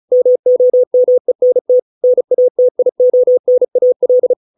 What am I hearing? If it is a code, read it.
Morse code, heard as a keyed tone, about 34 wpm, 500 Hz. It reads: MOMENT NATIONAL